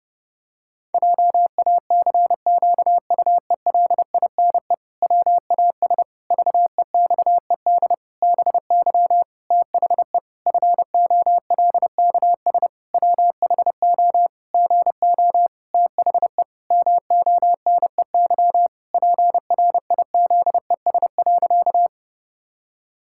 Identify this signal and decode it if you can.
Morse code, heard as a keyed tone, about 30 wpm, 700 Hz. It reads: JACQUELINE WAS VEXED BY THE FOLKS WHO GO THE MONEY PRIZES.
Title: Pangram Challenge v2 - 30wpm